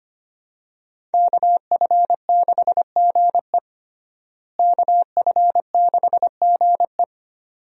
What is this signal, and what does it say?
Morse code, heard as a keyed tone, about 25 wpm, 700 Hz. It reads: KF6GE KF6GE
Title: USA Call Signs - 5 Characters - Repeated 2x - 25wpm